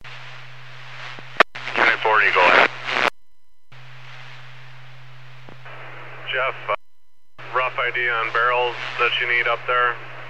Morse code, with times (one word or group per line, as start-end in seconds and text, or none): none